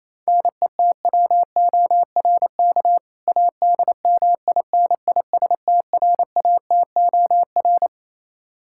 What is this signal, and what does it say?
Morse code, heard as a keyed tone, about 27 wpm, 700 Hz. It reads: NETWORK ADMINISTRATOR
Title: K1USN SST QSO Practice 28wpm